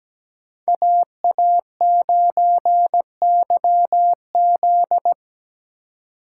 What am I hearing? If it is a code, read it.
Morse code, heard as a keyed tone, about 17 wpm, 700 Hz. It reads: AA9YZ
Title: Speed Racing - USA Call Signs 17wpm